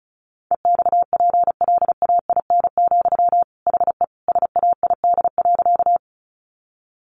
Morse code, heard as a keyed tone, about 34 wpm, 700 Hz. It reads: EXPLAIN, HE SAID.
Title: Sentences from Top 1100 Words 35wpm